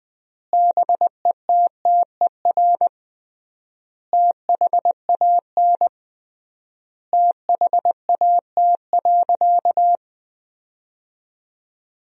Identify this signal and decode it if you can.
Morse code, heard as a keyed tone, about 21 wpm, 700 Hz. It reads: BETTER THAN THAT.
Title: 3x Word-Spacing - Sentences from Top 400 Words - 20wpm